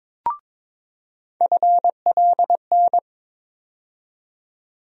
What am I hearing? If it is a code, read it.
Morse code, heard as a keyed tone, about 21 wpm, 700 Hz. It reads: FLN
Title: USA Call Sign Postfixes - 3 Letters 22wpm